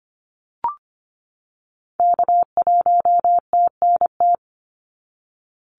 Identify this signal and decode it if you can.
Morse code, heard as a keyed tone, about 25 wpm, 700 Hz. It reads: K1TNT
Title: NA SOTA Activator Callsigns 25wpm